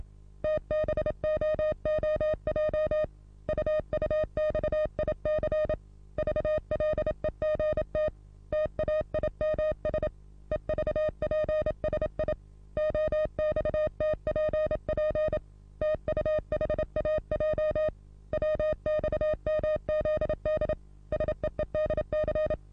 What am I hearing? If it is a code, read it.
Morse code, heard as a keyed tone, about 27 wpm, 620 Hz. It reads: TBOOJ UUXIC VLEGT TAIMS EVPSI OXTPP TUHAJ WXMZD SEEDC
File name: repos / Morsekurs / DJ4UF Morsen27